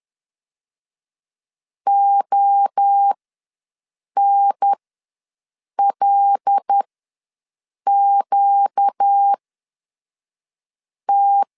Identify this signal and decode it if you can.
Morse code, heard as a keyed tone, about 10 wpm, spaced as for 5 wpm, 800 Hz. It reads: ONLQT
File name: LEZIONI / 5 LBVC / 5 L ws 10wpm 3636